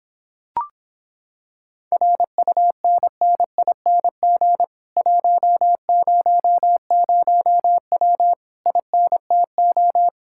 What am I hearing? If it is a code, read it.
Morse code, heard as a keyed tone, about 26 wpm, 700 Hz. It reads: RUNNING 100W INTO